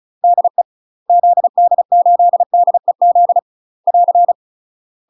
Morse code, heard as a keyed tone, about 35 wpm, 700 Hz. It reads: DE ZD8DEZ <AR>
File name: UK MASTER QSOs TRAINING/35WPM